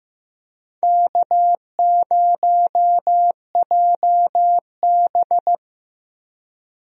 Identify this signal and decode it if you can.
Morse code, heard as a keyed tone, about 15 wpm, 700 Hz. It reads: K0JB
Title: USA Call Signs - 4 Characters - Repeated 3x - 15wpm